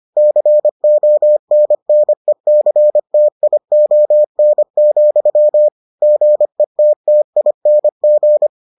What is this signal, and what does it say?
Morse code, heard as a keyed tone, about 25 wpm, 600 Hz. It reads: CONNECTION, GETTING